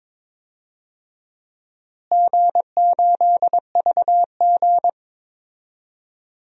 Morse code, heard as a keyed tone, about 22 wpm, 700 Hz. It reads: G8VG
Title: DXCC-100 Call Signs - 4 Characters - Repeated 3x - 22wpm